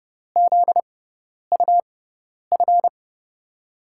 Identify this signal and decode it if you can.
Morse code, heard as a keyed tone, about 30 wpm, spaced as for 10 wpm, 700 Hz. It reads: ZUF
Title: USA Call Signs - 6 Characters - 6x Character Spacing - 30wpm